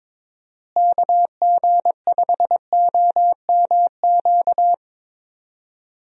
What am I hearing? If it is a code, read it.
Morse code, heard as a keyed tone, about 21 wpm, 700 Hz. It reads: KG5OMQ
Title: USA Call Signs - 6 Characters - Repeated 2x - 22wpm